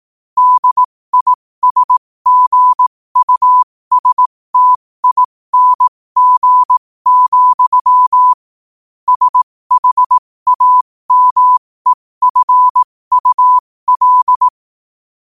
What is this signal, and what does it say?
Morse code, heard as a keyed tone, about 18 wpm, spaced as for 15 wpm, 1000 Hz. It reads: DISGUSTING, SHAMEFUL
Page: New quotes every day in morse code at 15 Words per minute.